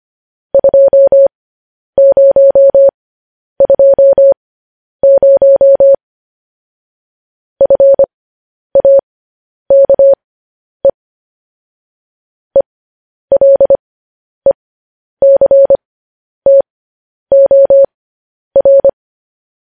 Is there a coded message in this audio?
Yes: Morse code from 0.5 to 18.9 s, heard as a keyed tone, about 25 wpm, spaced as for 10 wpm, 550 Hz. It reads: 2020 FAKE ELECTOR